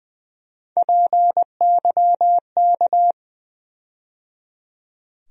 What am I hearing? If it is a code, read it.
Morse code, heard as a keyed tone, about 20 wpm, 700 Hz. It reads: PYK
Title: USA Call Sign Postfixes - 3 Letters 20wpm